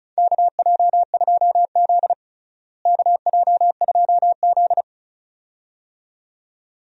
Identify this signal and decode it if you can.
Morse code, heard as a keyed tone, about 36 wpm, 700 Hz. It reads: KJ2Z KJ2Z
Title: USA Call Signs - 4 Characters - Repeated 2x - 35wpm